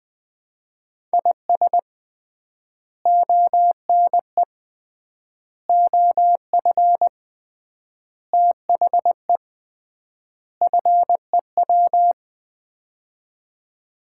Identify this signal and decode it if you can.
Morse code, heard as a keyed tone, about 20 wpm, 700 Hz. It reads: IS ONE OF THE FEW
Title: 3x Word-Spacing - Sets of 5 Words - 20wpm